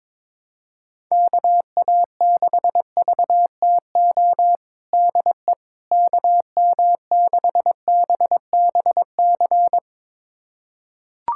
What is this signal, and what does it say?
Morse code, heard as a keyed tone, about 22 wpm, 700 Hz. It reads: KA6VTO DE KM6BBC
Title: USA Callsign DE Callsign 22wpm